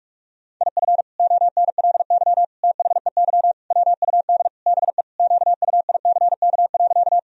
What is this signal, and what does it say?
Morse code, heard as a keyed tone, about 42 wpm, 700 Hz. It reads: IF ONLY THEY WUD BE QUICK.
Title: Sentences from Top 500 Words and Abbreviations 45wpm